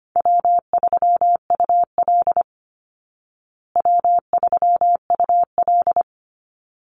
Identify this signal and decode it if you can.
Morse code, heard as a keyed tone, about 25 wpm, 700 Hz. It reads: W3UL W3UL
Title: Easy Call Signs Repeated 2x - 25wpm